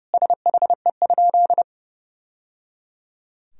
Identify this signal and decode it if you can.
Morse code, heard as a keyed tone, about 30 wpm, 700 Hz. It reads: SHE?
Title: Sentences from Top 1000 Words - Review Part 3 - 30wpm